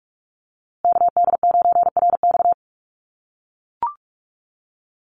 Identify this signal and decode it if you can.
Morse code, heard as a keyed tone, about 44 wpm, 700 Hz. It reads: KD9RX
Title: USA Call Signs Repeated 3x 45wpm